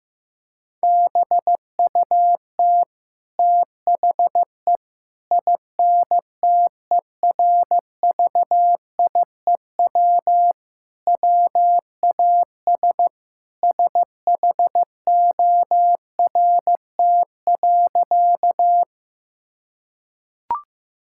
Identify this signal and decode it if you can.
Morse code, heard as a keyed tone, about 16 wpm, 700 Hz. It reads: BUT THE INTERVIEW WAS SHORT.
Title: Sentences from Top 1600 Words 15wpm